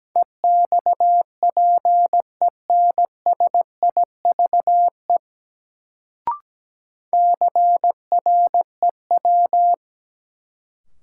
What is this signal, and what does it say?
Morse code, heard as a keyed tone, about 17 wpm, 700 Hz. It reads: EXPENSIVE CREW
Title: Sentences from Top 1500 Words - New Word Drill Alt - 17wpm